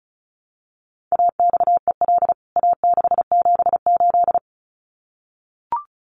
Morse code, heard as a keyed tone, about 34 wpm, 700 Hz. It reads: AXEL A678
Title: CWT Exchange 35wpm